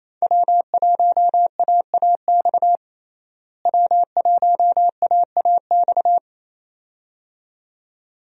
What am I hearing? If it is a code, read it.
Morse code, heard as a keyed tone, about 28 wpm, 700 Hz. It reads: W1AAX W1AAX
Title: USA Call Signs - 5 Characters - Repeated 2x - 28wpm